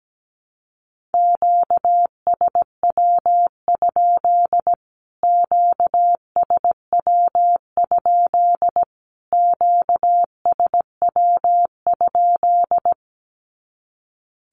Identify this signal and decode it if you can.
Morse code, heard as a keyed tone, about 17 wpm, 700 Hz. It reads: QSW? QSW? QSW?
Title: Q-Codes Repeated 3x 17wpm